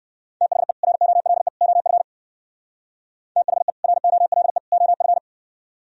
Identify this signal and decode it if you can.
Morse code, heard as a keyed tone, about 58 wpm, 700 Hz. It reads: THE ROLE OF THE ROLE OF